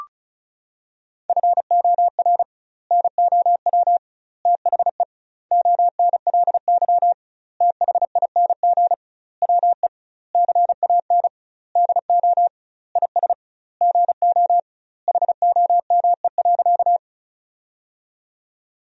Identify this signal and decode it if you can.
Morse code, heard as a keyed tone, about 36 wpm, 700 Hz. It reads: FOR NOW THE ONLY THING WE CAN DO IS GO HOME.
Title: 2x Word-Spacing - Sentences from Top 200 Words - 35wpm